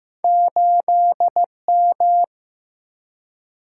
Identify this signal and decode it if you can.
Morse code, heard as a keyed tone, about 15 wpm, 700 Hz. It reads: 8M
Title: Single-Letter Single-Number 15wpm